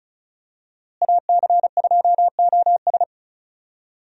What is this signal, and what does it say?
Morse code, heard as a keyed tone, about 36 wpm, 700 Hz. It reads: AC2OS
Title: USA Call Signs - 5 Characters - Repeated 2x - 35wpm